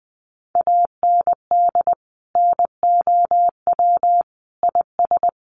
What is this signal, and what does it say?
Morse code, heard as a keyed tone, about 20 wpm, 700 Hz. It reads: AND NOW IS
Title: Sentences from Top 800 Words 20wpm